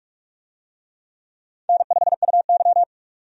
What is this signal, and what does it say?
Morse code, heard as a keyed tone, about 45 wpm, 700 Hz. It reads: N5UY
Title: USA Call Signs - 4 Characters - 45wpm